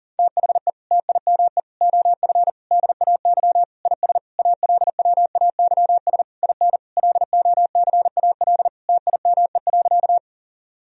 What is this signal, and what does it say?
Morse code, heard as a keyed tone, about 37 wpm, 700 Hz. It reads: THE TIME OF DAY IS ALWAYS IN LOCAL TIME.
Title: Sentences from Top 1000 Words - Review Part 3 - 40wpm